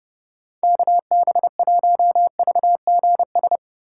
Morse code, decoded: KB1VGS

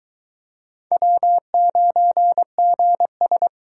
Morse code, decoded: W9GS

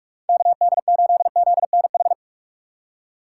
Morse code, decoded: KD8ZNH

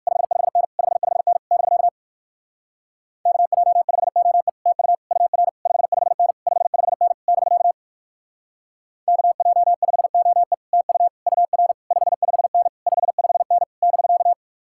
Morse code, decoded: 55N 55N <BK> KJ5OE TU UR 55N 55N <BK> KJ5OE TU UR 55N 55N <BK>

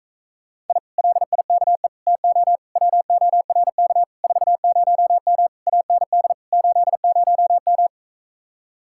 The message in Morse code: I LIKE TO WORK 40M AND 80M

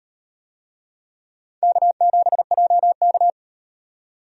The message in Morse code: K7JK

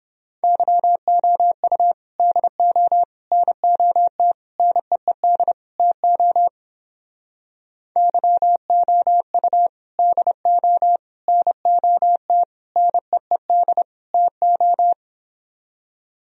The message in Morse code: YOU DO NOT NEED TO YOU DO NOT NEED TO

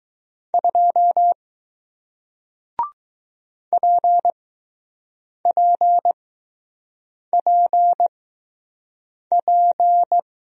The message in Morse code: 2 PPPP

